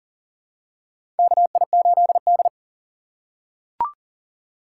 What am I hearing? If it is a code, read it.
Morse code, heard as a keyed tone, about 40 wpm, 700 Hz. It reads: KI8D
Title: USA Call Signs - 4 Characters - 6x Character Spacing - 40wpm